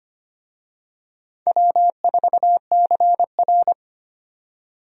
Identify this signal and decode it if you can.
Morse code, heard as a keyed tone, about 25 wpm, 700 Hz. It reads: W4CR
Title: USA Call Signs - 4 Characters - 25wpm